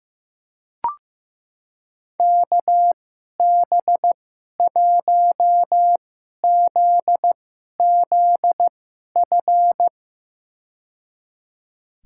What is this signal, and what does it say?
Morse code, heard as a keyed tone, about 15 wpm, spaced as for 11 wpm, 700 Hz. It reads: KB1ZZF
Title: USA Call Signs - 2x Character Spacing - 15wpm